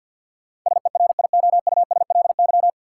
Morse code, heard as a keyed tone, about 52 wpm, 700 Hz. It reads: SERIOUSLY
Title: Sentences from Top 1200 Words - New Word Drill - 50wpm